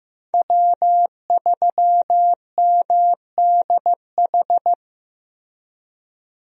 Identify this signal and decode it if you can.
Morse code, heard as a keyed tone, about 15 wpm, 700 Hz. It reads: W3MDH
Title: USA Call Signs - 5 Characters - Repeated 3x - 15wpm